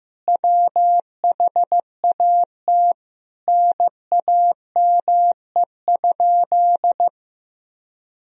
Morse code, heard as a keyed tone, about 16 wpm, 700 Hz. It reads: WHAT NAME?
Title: Sentences from Top 1000 Words - Review Part 1 - 15wpm